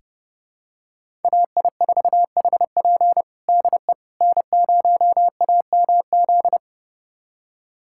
Morse code, heard as a keyed tone, about 29 wpm, 700 Hz. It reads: AI4HP DE N0AMZ
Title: USA Callsign DE Callsign - 5 Characters - 30wpm